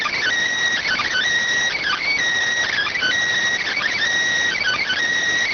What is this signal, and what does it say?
Emergency Alert System (SAME) tones, heard as harsh buzzing bursts. It is heard from 0.8 to 5.0 s.